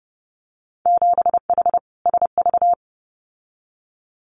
Morse code, heard as a keyed tone, about 30 wpm, 700 Hz. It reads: 7H SV
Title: Winter Field Day Exchanges Encore 30wpm